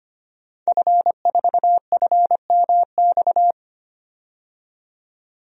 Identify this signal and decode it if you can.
Morse code, heard as a keyed tone, about 25 wpm, 700 Hz. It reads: F4FMX
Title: DXCC-100 Call Signs - 5 Characters - 25wpm